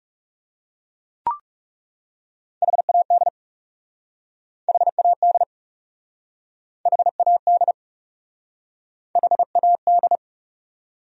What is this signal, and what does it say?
Morse code, heard as a keyed tone, about 39 wpm, 700 Hz. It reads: HAD HAD HAD HAD